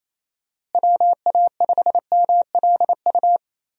Morse code, heard as a keyed tone, about 28 wpm, 700 Hz. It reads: WA5MLU
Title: USA Call Signs - 6 Characters - Repeated 3x - 28wpm